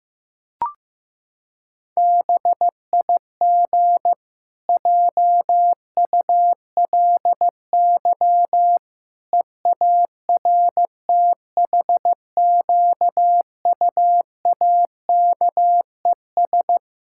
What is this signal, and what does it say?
Morse code, heard as a keyed tone, about 15 wpm, 700 Hz. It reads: BIG JULY EARTHQUAKES
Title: Pangram Challenge v2 - 15wpm